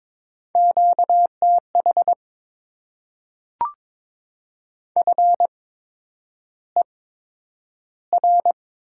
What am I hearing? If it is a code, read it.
Morse code, heard as a keyed tone, about 22 wpm, spaced as for 9 wpm, 700 Hz. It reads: QTH FER
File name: Top 20 QSO Elements - 8x Character Spacing - 22wpm